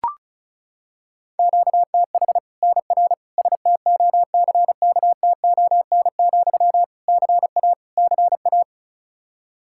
Morse code, heard as a keyed tone, about 34 wpm, 700 Hz. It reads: QTH NR STOCKTON, CA CA
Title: Practice QSO - Exchange 1 - Beginning - 35wpm